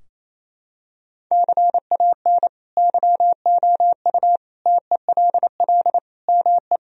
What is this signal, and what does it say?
Morse code, heard as a keyed tone, about 28 wpm, 700 Hz. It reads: CAN YOU TELL ME
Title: Sentences from Top 1000 Words - Review Part 3 - 28wpm